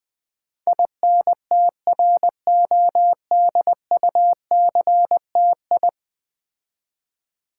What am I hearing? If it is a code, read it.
Morse code, heard as a keyed tone, about 20 wpm, 700 Hz. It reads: INTRODUCTI
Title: Build A Word - 5 Letters At A Time - 12-Letter Words - 20wpm